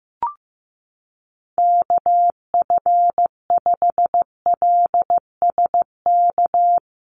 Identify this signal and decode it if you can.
Morse code, heard as a keyed tone, about 14 wpm, 700 Hz. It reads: KF5LSK
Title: USA Call Signs 15wpm